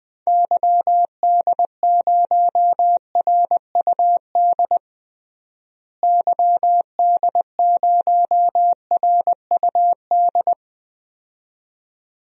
Morse code, heard as a keyed tone, about 20 wpm, 700 Hz. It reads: YD0RUD YD0RUD
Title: DXCC-100 Call Signs - 6 Characters - Repeated 2x - 20wpm